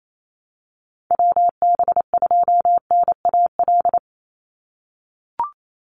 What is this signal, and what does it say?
Morse code, heard as a keyed tone, about 28 wpm, 700 Hz. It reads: WB2NAL
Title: USA Call Signs 28wpm